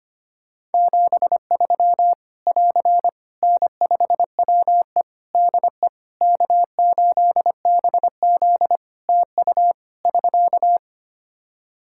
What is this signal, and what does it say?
Morse code, heard as a keyed tone, about 25 wpm, 700 Hz. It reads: 73 <AR> N5WE DE K8BZ TU <SK>
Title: K1USN SST QSO Practice 25wpm